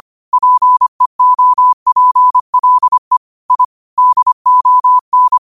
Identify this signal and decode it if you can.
Morse code, heard as a keyed tone, about 25 wpm, 1000 Hz. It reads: PEOPLE I DON
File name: qotdfiles / 25wpm